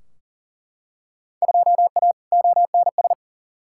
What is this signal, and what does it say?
Morse code, heard as a keyed tone, about 40 wpm, 700 Hz. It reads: 2A ONS
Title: Field Day Exchanges Encore - 40wpm